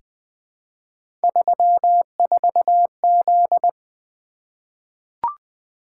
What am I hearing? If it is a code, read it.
Morse code, heard as a keyed tone, about 20 wpm, 700 Hz. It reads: 34Z
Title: Three Letter-Number Combinations - 20wpm